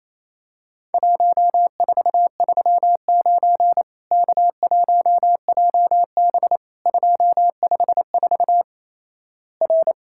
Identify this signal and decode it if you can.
Morse code, heard as a keyed tone, about 28 wpm, 700 Hz. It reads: 1439 K1JB 254 R
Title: Worked All Europe DX Contest - QTC Exchange - 28wpm